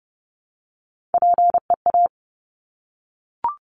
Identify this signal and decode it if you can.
Morse code, heard as a keyed tone, about 30 wpm, 700 Hz. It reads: PEA
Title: Top Three Letter Combinations in Words 30wpm